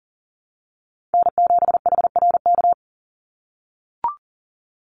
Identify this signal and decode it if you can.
Morse code, heard as a keyed tone, about 40 wpm, 700 Hz. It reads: N7HRK
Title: Call Signs - 40wpm